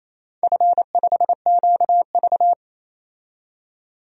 Morse code, heard as a keyed tone, about 28 wpm, 700 Hz. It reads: F5QV
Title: DXCC-100 Call Signs - 4 Characters - 28wpm